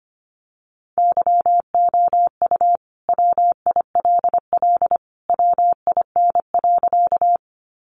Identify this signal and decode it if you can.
Morse code, heard as a keyed tone, about 25 wpm, 700 Hz. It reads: YOU WILL WIN.
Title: Sentences from Top 1500 Words 25wpm